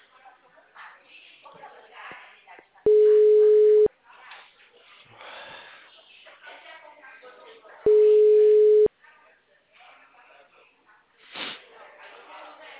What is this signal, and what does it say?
Ringback tone (Europe, 425 Hz), with speech.
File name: out- -2003- - - .48